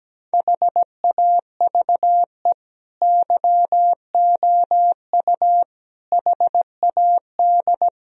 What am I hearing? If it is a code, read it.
Morse code, heard as a keyed tone, about 17 wpm, 700 Hz. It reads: HAVE YOU HAD
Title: Sentences from Top 1000 Words - Review Part 3 - 17wpm